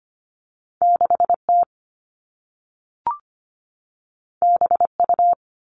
Morse code, heard as a keyed tone, about 25 wpm, 700 Hz. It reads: 6T BU
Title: All Cut Numbers - 2-Digit 25wpm